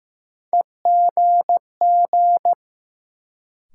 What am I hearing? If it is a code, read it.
Morse code, heard as a keyed tone, about 15 wpm, 700 Hz. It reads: EGG
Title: Sentences from Top 1000 Words - Review Part 7 - New Word Drill - 15wpm